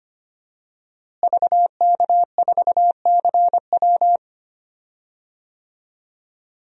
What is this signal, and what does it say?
Morse code, heard as a keyed tone, about 25 wpm, 700 Hz. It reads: VK4CW
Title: DXCC-100 Call Signs - 5 Characters - Repeated 2x - 25wpm